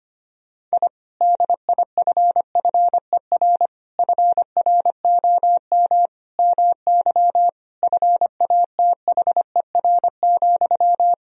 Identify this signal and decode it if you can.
Morse code, heard as a keyed tone, about 25 wpm, 700 Hz. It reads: I DIFFER FROM MY FATHER,
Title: Sentences from Top 1000 Words - Review Part 1 - 25wpm